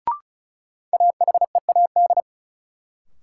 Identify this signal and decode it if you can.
Morse code, heard as a keyed tone, about 36 wpm, 700 Hz. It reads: AHEAD
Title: Sentences from Top 1000 Words - Review Part 6 - New Word Drill - 35wpm